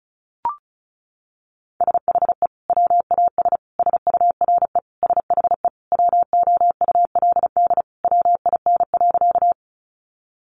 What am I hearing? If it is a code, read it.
Morse code, heard as a keyed tone, about 34 wpm, 700 Hz. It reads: SHE WAS SURE SHE WOULD WIN.
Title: Sentences from Top 1500 Words 35wpm